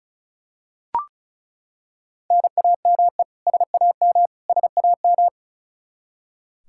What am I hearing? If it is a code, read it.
Morse code, heard as a keyed tone, about 36 wpm, 700 Hz. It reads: NAME SAM SAM
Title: Practice QSO - Exchange 1 - Beginning - 35wpm